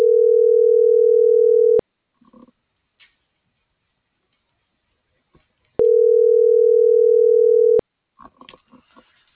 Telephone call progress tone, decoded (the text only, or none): ringback tone